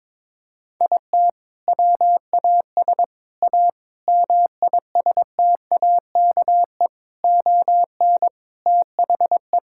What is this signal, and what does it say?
Morse code, heard as a keyed tone, about 22 wpm, 700 Hz. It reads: IT WAS A MISTAKE ON THE
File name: practice / 22wpm / Sentences with 4-Word Phrases - 22wpm